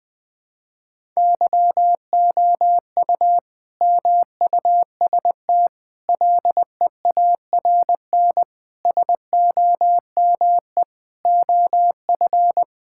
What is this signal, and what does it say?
Morse code, heard as a keyed tone, about 20 wpm, 700 Hz. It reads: YOU MUST LEARN SOME OF
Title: Sentences from Top 1000 Words - Review Part 6 - 20wpm